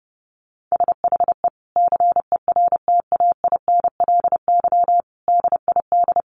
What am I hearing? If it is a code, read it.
Morse code, heard as a keyed tone, about 30 wpm, 700 Hz. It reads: SHE CERTAINLY DID
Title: Sentences from Top 1100 Words 30wpm